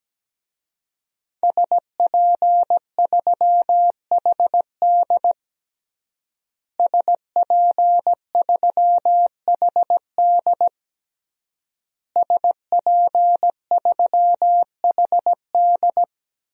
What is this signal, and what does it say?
Morse code, heard as a keyed tone, about 17 wpm, 700 Hz. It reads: SP3HD SP3HD SP3HD
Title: DXCC-100 Call Signs - 5 Characters - Repeated 3x - 17wpm